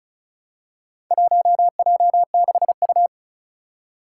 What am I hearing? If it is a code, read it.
Morse code, heard as a keyed tone, about 34 wpm, 700 Hz. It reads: 1J6U
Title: Four Challenging Letter-Number Combinations 35wpm